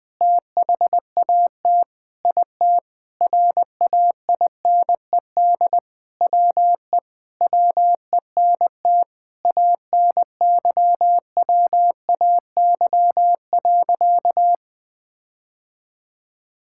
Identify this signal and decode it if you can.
Morse code, heard as a keyed tone, about 20 wpm, 700 Hz. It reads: THAT IT RAINED WE WENT ANYWAY.
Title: Sentences with 4-Word Phrases - 20wpm